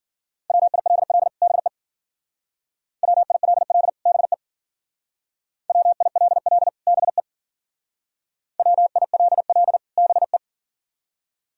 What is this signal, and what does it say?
Morse code, heard as a keyed tone, about 52 wpm, 700 Hz. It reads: WILL BE WILL BE WILL BE WILL BE